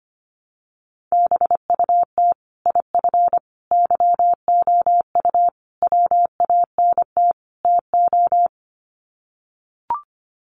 Morse code, heard as a keyed tone, about 25 wpm, 700 Hz. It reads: BUT IF YOU WANT TO
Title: Sets of 5 Words Encore2 25wpm